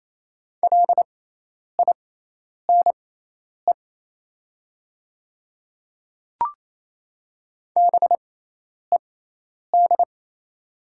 Morse code, heard as a keyed tone, about 28 wpm, spaced as for 10 wpm, 700 Hz. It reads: LINE BED
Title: 6x Character Spacing - Top 500 Words - 28wpm